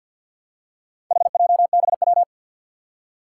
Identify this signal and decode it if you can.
Morse code, heard as a keyed tone, about 53 wpm, 700 Hz. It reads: HJBW
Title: Four Challenging Letter-Number Combinations 50wpm